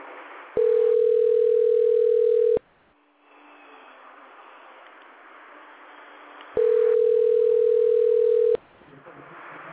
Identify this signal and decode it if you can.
Ringback tone (North America), 440 + 480 Hz.